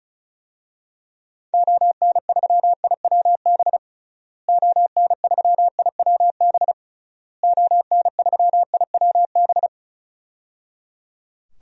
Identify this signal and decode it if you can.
Morse code, heard as a keyed tone, about 36 wpm, 700 Hz. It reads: ON3IWB ON3IWB ON3IWB
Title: DXCC-100 Call Signs - 6 Characters - Repeated 3x - 35wpm